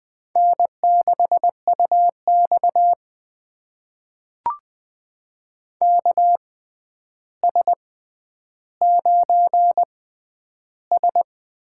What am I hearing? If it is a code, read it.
Morse code, heard as a keyed tone, about 21 wpm, spaced as for 10 wpm, 700 Hz. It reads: N6UX KS9S
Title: USA Call Signs - 4 Characters - 6x Character Spacing - 20wpm